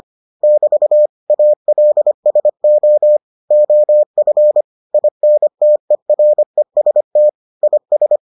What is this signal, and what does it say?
Morse code, heard as a keyed tone, about 25 wpm, 600 Hz. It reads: <BT> ALSO OF INTEREST IS